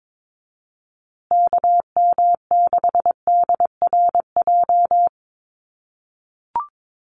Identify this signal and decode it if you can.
Morse code, heard as a keyed tone, about 22 wpm, 700 Hz. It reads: KM6DRJ
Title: USA Call Signs Repeated 3x 22wpm